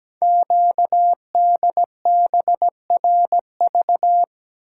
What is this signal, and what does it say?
Morse code, heard as a keyed tone, about 17 wpm, 700 Hz. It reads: QDBRV
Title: Code Groups - 5 Letters - 17wpm